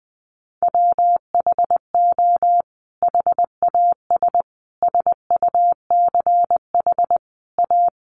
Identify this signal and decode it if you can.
Morse code, heard as a keyed tone, about 20 wpm, 700 Hz. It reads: WHO HAS SUCH A
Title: Sentences from Top 1200 Words 20wpm